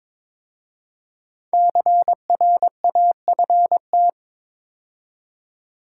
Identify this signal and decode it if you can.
Morse code, heard as a keyed tone, about 22 wpm, 690 Hz. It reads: CRAFT
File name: 5 Letter Words - First 3 Letters Given - 22wpm